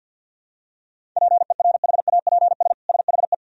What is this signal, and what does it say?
Morse code, heard as a keyed tone, about 47 wpm, 700 Hz. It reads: PERHAPS SHE
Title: Sentences from Top 1000 Words - Review Part 1 - 50wpm